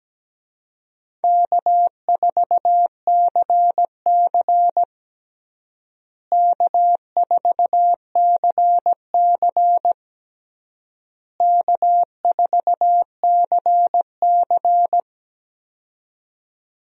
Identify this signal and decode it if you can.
Morse code, heard as a keyed tone, about 17 wpm, 700 Hz. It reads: K4CC K4CC K4CC
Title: USA Call Signs - 4 Characters - Repeated 3x - 17wpm